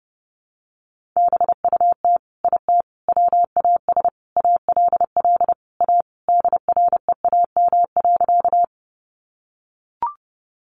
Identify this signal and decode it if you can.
Morse code, heard as a keyed tone, about 30 wpm, 700 Hz. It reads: BUT IT WAS ALL A DREAM.
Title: Sentences from Top 1500 Words 30wpm